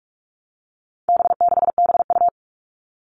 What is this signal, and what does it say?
Morse code, heard as a keyed tone, about 44 wpm, 700 Hz. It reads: B6BU
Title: All Cut Numbers - 4-Digit 45wpm